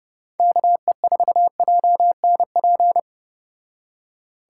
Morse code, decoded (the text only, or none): KE4JNP